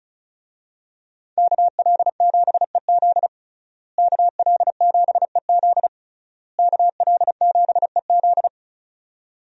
Morse code, heard as a keyed tone, about 36 wpm, 700 Hz. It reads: KL7EZ KL7EZ KL7EZ
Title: DXCC-100 Call Signs - 5 Characters - Repeated 3x - 35wpm